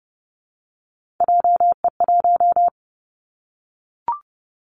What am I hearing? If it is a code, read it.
Morse code, heard as a keyed tone, about 30 wpm, 700 Hz. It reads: JE1
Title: Callsign Prefixes for Top 25 DXCC - Repeated 3x - 30wpm